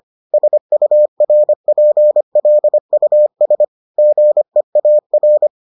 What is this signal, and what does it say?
Morse code, heard as a keyed tone, about 25 wpm, 600 Hz. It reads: SURPLUS GEAR